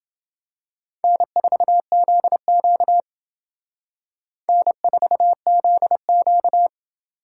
Morse code, heard as a keyed tone, about 28 wpm, 700 Hz. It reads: N4ZQ N4ZQ